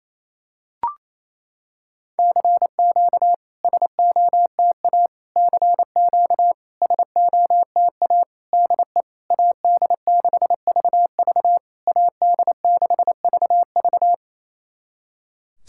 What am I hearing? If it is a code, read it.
Morse code, heard as a keyed tone, about 28 wpm, 700 Hz. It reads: CQ SOTA CQ SOTA DE AD6VV AD6VV
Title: SOTA QSO Practice 28wpm